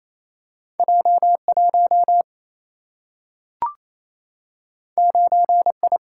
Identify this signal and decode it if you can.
Morse code, heard as a keyed tone, about 28 wpm, 700 Hz. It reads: J1 9I
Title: Single-Letter Single-Number 28wpm